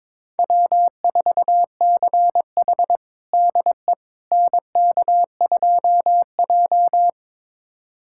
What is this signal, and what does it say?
Morse code, heard as a keyed tone, about 22 wpm, 700 Hz. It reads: W4CH DE NK2J
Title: USA Callsign DE Callsign - 4 Characters - 22wpm